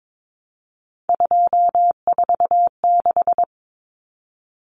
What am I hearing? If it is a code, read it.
Morse code, heard as a keyed tone, about 22 wpm, 700 Hz. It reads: 246
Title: Numbers - 3-Digit 22wpm